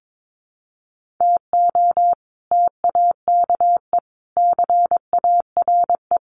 Morse code, heard as a keyed tone, about 22 wpm, 700 Hz. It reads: TO TAKE CARE
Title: Sets of 4 Words 22wpm